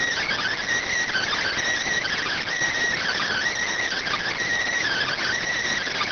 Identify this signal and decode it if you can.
Emergency Alert System (SAME) tones (header bursts), heard as harsh buzzing bursts. It is heard from 1.0 to 5.4 s.